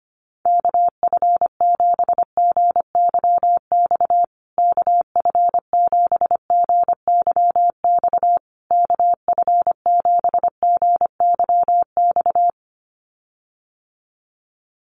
Morse code, heard as a keyed tone, about 25 wpm, 700 Hz. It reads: KF7GYX KF7GYX KF7GYX
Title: USA Call Signs Repeated 3x 25wpm